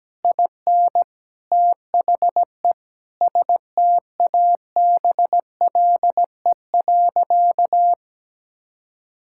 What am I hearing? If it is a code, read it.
Morse code, heard as a keyed tone, about 17 wpm, 700 Hz. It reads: IN THE STABLE.
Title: Sentences from Top 1600 Words 17wpm